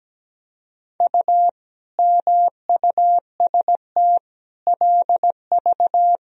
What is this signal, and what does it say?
Morse code, heard as a keyed tone, about 17 wpm, 700 Hz. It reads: U MUST LV
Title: Sentences from Top 400 Words and Abbreviations 17wpm